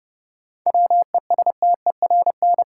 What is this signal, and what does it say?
Morse code, heard as a keyed tone, about 29 wpm, 700 Hz. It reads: WESTERN
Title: Sentences from Top 1000 Words - Review Part 7 - New Word Drill - 30wpm